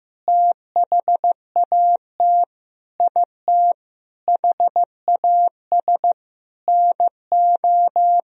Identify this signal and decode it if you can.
Morse code, heard as a keyed tone, about 15 wpm, 700 Hz. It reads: THAT IT HAS NO
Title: Sentences from Top 1000 Words - Review Part 4 - 15wpm